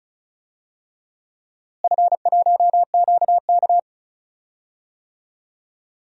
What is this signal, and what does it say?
Morse code, heard as a keyed tone, about 36 wpm, 700 Hz. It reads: F1QK
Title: DXCC-100 Call Signs - 4 Characters - Repeated 2x - 35wpm